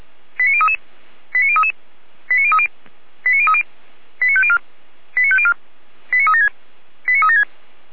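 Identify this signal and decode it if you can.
Five-tone selective call (ZVEI), heard as quick runs of five tones, tones of about 70 ms. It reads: 89033 89033 89030 89030 89504 89504 89376 89376
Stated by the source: ZVEI five-tone call received by SDR, on the audio server of Feuerwehr Orlamünde (fire brigade).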